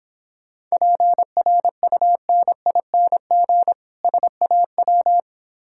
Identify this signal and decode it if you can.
Morse code, heard as a keyed tone, about 26 wpm, 700 Hz. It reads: PRUNING SAW